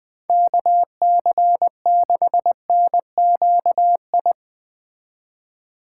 Morse code, heard as a keyed tone, about 20 wpm, 700 Hz. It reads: KC6NQI